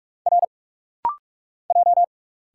Morse code, heard as a keyed tone, about 45 wpm, 690 Hz. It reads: R J
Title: Single Letters - Rapid-Fire 45wpm